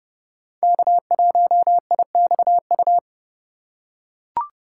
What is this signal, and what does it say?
Morse code, heard as a keyed tone, about 32 wpm, 700 Hz. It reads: K1IXU
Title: USA Call Signs - 5 Characters - 3x Character Spacing - 30wpm